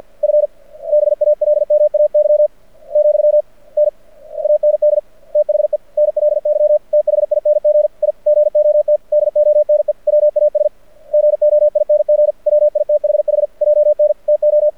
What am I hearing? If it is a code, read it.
Morse code, heard as a keyed tone, about 53 wpm, 600 Hz. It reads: F BALMAY , N AND THE NLY THING I GOT DONE WAS GOING WITHH JN TO
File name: archived / CRSnet